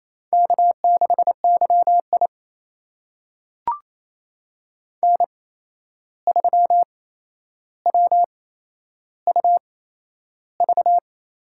Morse code, heard as a keyed tone, about 28 wpm, spaced as for 11 wpm, 700 Hz. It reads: K6YI N3WUV
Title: USA Call Signs - 8x Character Spacing - 28wpm